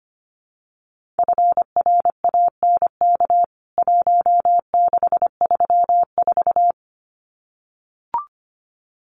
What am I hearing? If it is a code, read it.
Morse code, heard as a keyed tone, about 25 wpm, 700 Hz. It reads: FRANK 1634
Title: CWT Exchange 25wpm